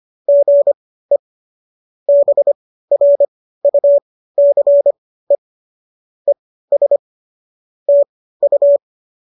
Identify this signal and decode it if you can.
Morse code, heard as a keyed tone, about 25 wpm, spaced as for 15 wpm, 570 Hz. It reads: GE BRUCE ES TU
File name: qso201 15